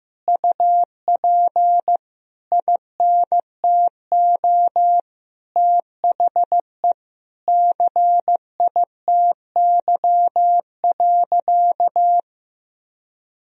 Morse code, heard as a keyed tone, about 15 wpm, 700 Hz. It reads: UP INTO THE CITY.
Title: Sentences from Top 1000 Words - Review Part 1 - 15wpm